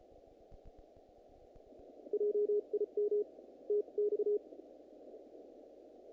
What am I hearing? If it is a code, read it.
Morse code, heard as a keyed tone, about 34 wpm, spaced as for 30 wpm, 420 Hz. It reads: JIM TX
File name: contests / naqp-cw-k5tr-op-k5pi- -1900